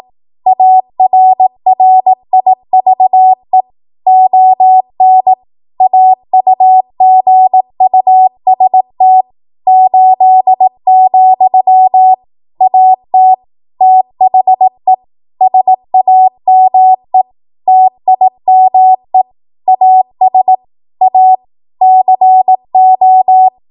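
Morse code, heard as a keyed tone, about 18 wpm, 750 Hz. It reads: ARRIVE ON AUGUST 8, AT THE SAME TIME AS A CO
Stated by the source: ARRL 18 wpm Morse code practice archive transmission.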